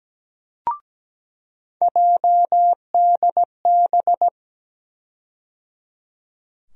Morse code, heard as a keyed tone, about 17 wpm, 700 Hz. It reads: JDB
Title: USA Call Sign Postfixes - 3 Letters 17wpm